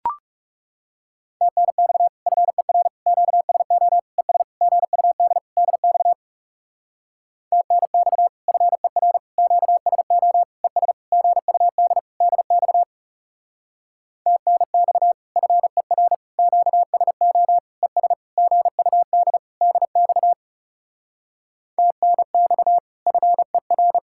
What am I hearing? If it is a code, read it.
Morse code, heard as a keyed tone, about 38 wpm, 700 Hz. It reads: TNX FER QSO ES GUD DX TNX FER QSO ES GUD DX TNX FER QSO ES GUD DX TNX FER